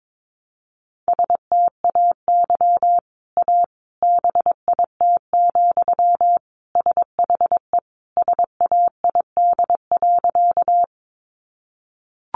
Morse code, heard as a keyed tone, about 22 wpm, 700 Hz. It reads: STAY A BIT, SHE SAID.
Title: Sentences from Top 900 Words 22wpm